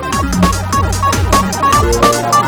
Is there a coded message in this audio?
No.